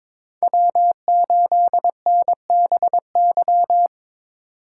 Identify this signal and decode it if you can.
Morse code, heard as a keyed tone, about 26 wpm, 700 Hz. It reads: W8NBY
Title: Speed Racing - USA 1x3 Call Signs 22wpm